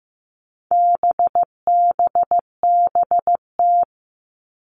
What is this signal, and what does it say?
Morse code, heard as a keyed tone, about 15 wpm, 700 Hz. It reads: BBBT
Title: All Cut Numbers - 4-Digit 15wpm